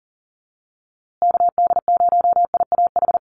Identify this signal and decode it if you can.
Morse code, heard as a keyed tone, about 40 wpm, 700 Hz. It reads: KD0IAH
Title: USA Call Signs Repeated 2x 40wpm